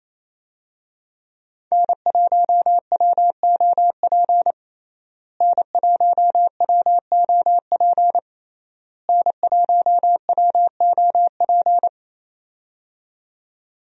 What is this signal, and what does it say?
Morse code, heard as a keyed tone, about 28 wpm, 700 Hz. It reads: N1WOP N1WOP N1WOP
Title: USA Call Signs - 5 Characters - Repeated 3x - 28wpm